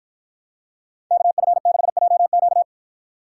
Morse code, heard as a keyed tone, about 49 wpm, 700 Hz. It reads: KF6JQ